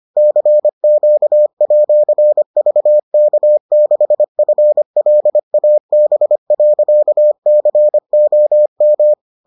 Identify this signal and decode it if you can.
Morse code, heard as a keyed tone, about 25 wpm, 600 Hz. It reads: CQ@VK6FLAB.COM